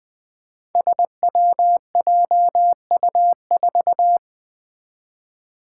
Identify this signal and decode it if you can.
Morse code, heard as a keyed tone, about 20 wpm, 700 Hz. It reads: SWJU4
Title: Five Challenging Letter-Number Combinations 20wpm